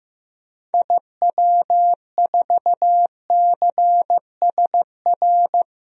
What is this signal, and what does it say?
Morse code, heard as a keyed tone, about 15 wpm, 700 Hz. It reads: IW4CSR
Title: DXCC-100 Call Signs - 6 Characters - Repeated 3x - 15wpm